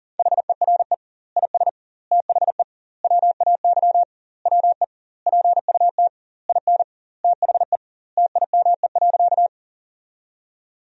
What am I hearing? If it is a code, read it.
Morse code, heard as a keyed tone, about 40 wpm, 700 Hz. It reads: HERE IS THE WAY WE PUT IN THE TIME.
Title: 2x Word-Spacing - Sentences from Top 200 Words - 40wpm